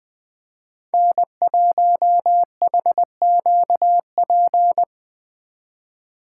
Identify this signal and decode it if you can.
Morse code, heard as a keyed tone, about 20 wpm, 700 Hz. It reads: N1HQP